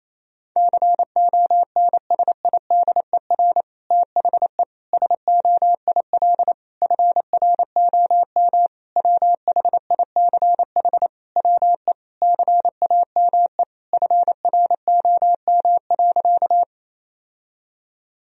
Morse code, heard as a keyed tone, about 28 wpm, 700 Hz. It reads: CONSIDER THE SOIL FROM WHICH WE CAME FROM.
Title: Sentences from Top 1000 Words - Review Part 4 - 28wpm